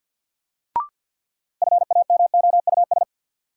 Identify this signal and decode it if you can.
Morse code, heard as a keyed tone, about 52 wpm, 700 Hz. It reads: FAMOUS
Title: Sentences from Top 1500 Words - New Word Drill Alt - 50wpm